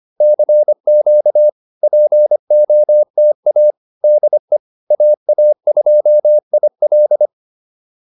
Morse code, heard as a keyed tone, about 25 wpm, 590 Hz. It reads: CQ POTA DE AA2IL